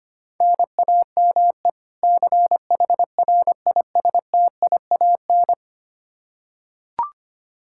Morse code, heard as a keyed tone, about 25 wpm, 700 Hz. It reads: NAME CHRISTIAN
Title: K1USN SST QSO Practice 25wpm